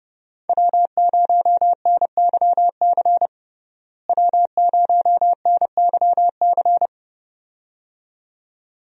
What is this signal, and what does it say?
Morse code, heard as a keyed tone, about 31 wpm, 700 Hz. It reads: W0NYC W0NYC
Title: USA Call Signs - 5 Characters - Repeated 2x - 30wpm